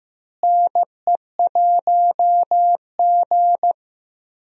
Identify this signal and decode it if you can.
Morse code, heard as a keyed tone, about 15 wpm, 700 Hz. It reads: NE1G
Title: USA Call Signs - 4 Characters - Repeated 2x - 15wpm